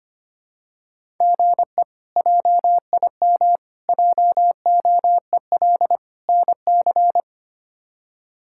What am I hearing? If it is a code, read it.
Morse code, heard as a keyed tone, about 25 wpm, 700 Hz. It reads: GE JIM JOEL NC
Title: K1USN SST QSO Practice - 25wpm